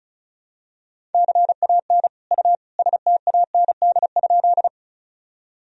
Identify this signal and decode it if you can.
Morse code, heard as a keyed tone, about 36 wpm, 700 Hz. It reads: CAN U STAND?
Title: Sentences from Top 300 Words and Abbreviations 35wpm